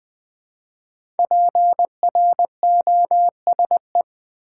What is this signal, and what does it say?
Morse code, heard as a keyed tone, about 20 wpm, 700 Hz. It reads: PROSE
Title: Build A Word - 5 Letters At A Time - 10-Letter Words - 20wpm